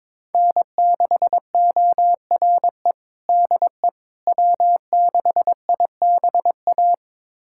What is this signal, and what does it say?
Morse code, heard as a keyed tone, about 21 wpm, 700 Hz. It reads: N6ORE DE W6IBA
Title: USA Callsign DE Callsign - 5 Characters - 22wpm